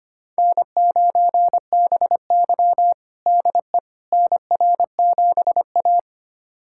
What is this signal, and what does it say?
Morse code, heard as a keyed tone, about 25 wpm, 700 Hz. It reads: N9BY DE NR7A
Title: USA Callsign DE Callsign - 4 Characters - 25wpm